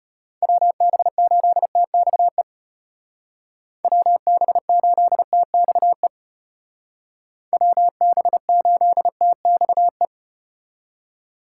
Morse code, heard as a keyed tone, about 33 wpm, 700 Hz. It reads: WB8TXE WB8TXE WB8TXE